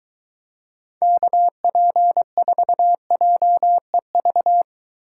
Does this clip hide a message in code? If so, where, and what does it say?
Yes: Morse code from 1.0 to 4.6 s, heard as a keyed tone, about 23 wpm, 700 Hz. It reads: KP4JEV